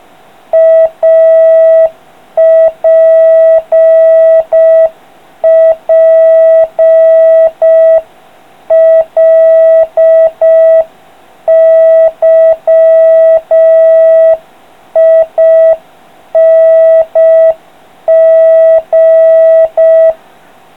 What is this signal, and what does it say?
Morse code, heard as a keyed tone, about 5 wpm, 3250 Hz, with noise. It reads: APPLYING